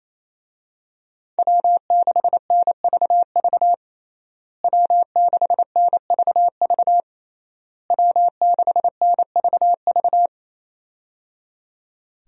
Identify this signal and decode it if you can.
Morse code, heard as a keyed tone, about 28 wpm, 700 Hz. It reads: W6NVV W6NVV W6NVV
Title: USA Call Signs - 5 Characters - Repeated 3x - 28wpm